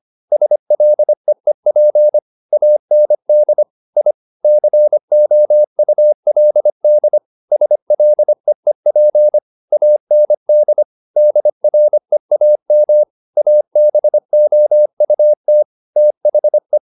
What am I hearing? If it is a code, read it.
Morse code, heard as a keyed tone, about 25 wpm, 600 Hz. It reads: SLEEP AND I COULD SLEEP AND DREAM ABOUT THE
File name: the old man and the sea part 1 25wpm